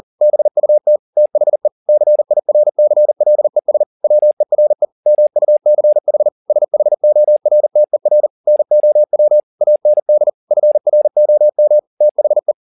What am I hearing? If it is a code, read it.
Morse code, heard as a keyed tone, about 40 wpm, 600 Hz. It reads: BUT THE CIRCLES WERE MUCH SHORTER NOW AND FROM THE
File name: the old man and the sea 2 40wpm